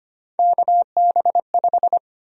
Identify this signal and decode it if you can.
Morse code, heard as a keyed tone, about 25 wpm, 700 Hz. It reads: KB5
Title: USA Call Sign Prefixes - Repeated 2x - 25wpm